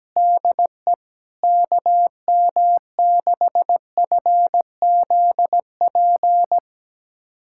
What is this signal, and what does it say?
Morse code, heard as a keyed tone, about 17 wpm, 700 Hz. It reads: DE KM6FZP
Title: USA Callsign DE Callsign - 6 Characters - 17wpm